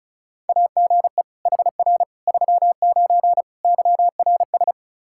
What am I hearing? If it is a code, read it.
Morse code, heard as a keyed tone, about 36 wpm, 700 Hz. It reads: AGE HR 39 YRS
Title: Practice QSO - Exchange 3 - More Info - 35wpm